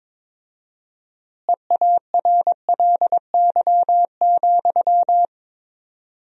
Morse code, heard as a keyed tone, about 22 wpm, 700 Hz. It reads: EARLY,
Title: 4x Word-Spacing - Sentences from Top 400 Words - 22wpm